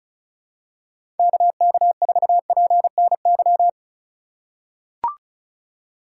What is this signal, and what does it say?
Morse code, heard as a keyed tone, about 36 wpm, 700 Hz. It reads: KK4PNY
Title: USA Call Signs - 2x Character Spacing - 35wpm